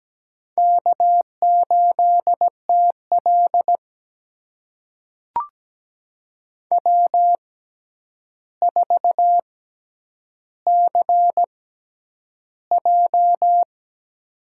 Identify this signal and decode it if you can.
Morse code, heard as a keyed tone, about 17 wpm, spaced as for 9 wpm, 700 Hz. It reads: K8TL W4CJ
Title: USA Call Signs - 4 Characters - 6x Character Spacing - 17wpm